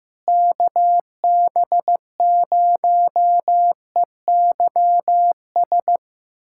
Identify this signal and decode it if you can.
Morse code, heard as a keyed tone, about 15 wpm, 700 Hz. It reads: KB0EYS
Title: USA Callsign DE Callsign - 6 Characters - 15wpm